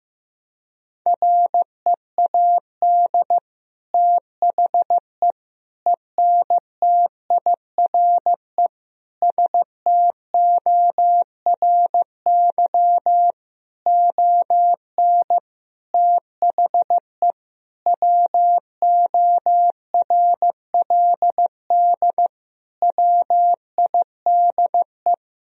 Morse code, heard as a keyed tone, about 15 wpm, 700 Hz. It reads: READ THE ENTIRE STORY ON THE WORLD WIDE
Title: Sentences from Top 1000 Words - Review Part 4 - 15wpm